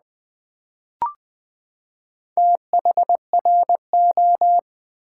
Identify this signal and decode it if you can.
Morse code, heard as a keyed tone, about 20 wpm, 700 Hz. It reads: THRO
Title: Build A Word - 4 Letters At A Time - 10-Letter Words - 20wpm